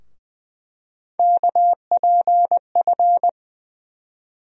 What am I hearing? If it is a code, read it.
Morse code, heard as a keyed tone, about 20 wpm, 700 Hz. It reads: KPF
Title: USA Call Sign Postfixes - 3 Letters 20wpm